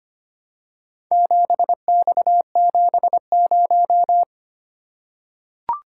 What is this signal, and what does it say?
Morse code, heard as a keyed tone, about 25 wpm, 700 Hz. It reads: 7X70
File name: practice / 25wpm / Four Letter-Number Combinations - 25wpm